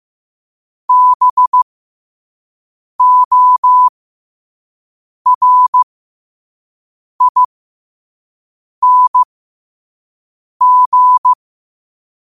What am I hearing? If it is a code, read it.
Morse code, heard as a keyed tone, about 15 wpm, spaced as for 5 wpm, 1000 Hz. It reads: BORING
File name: qotdfiles / 5wpm